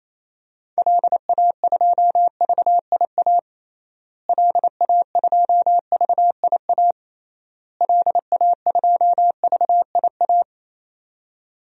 Morse code, heard as a keyed tone, about 27 wpm, 700 Hz. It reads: LA2VIA LA2VIA LA2VIA
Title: DXCC-100 Call Signs - 6 Characters - Repeated 3x - 28wpm